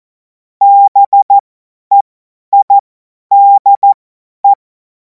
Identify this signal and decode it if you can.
Morse code, heard as a keyed tone, about 14 wpm, spaced as for 10 wpm, 800 Hz. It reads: BEIDE
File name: Morsekurs / Witze / Witz 05. Mai 2026 50 F70